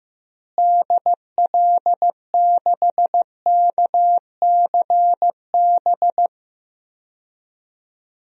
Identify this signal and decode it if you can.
Morse code, heard as a keyed tone, about 15 wpm, 700 Hz. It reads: DL6KCB
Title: DXCC-100 Call Signs - 6 Characters - Repeated 3x - 15wpm